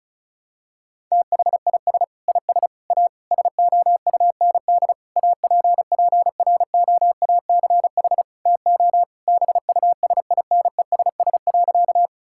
Morse code, heard as a keyed tone, about 36 wpm, 700 Hz. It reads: THIS IS A SOUND APPROACH TO BUSINESS.
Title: Sentences from Top 1000 Words - Review Part 6 - 35wpm